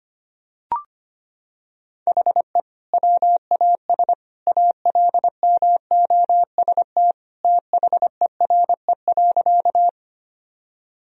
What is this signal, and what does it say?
Morse code, heard as a keyed tone, about 25 wpm, 700 Hz. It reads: HE WAS ALMOST THERE.
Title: Sentences from Top 1000 Words - Review Part 4 - 25wpm